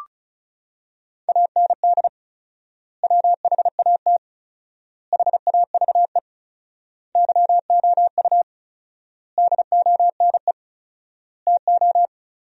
Morse code, heard as a keyed tone, about 36 wpm, 700 Hz. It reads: AND WHAT HAVE YOU DONE TO
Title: 4x Word-Spacing - Sentences from Top 500 Words - 35wpm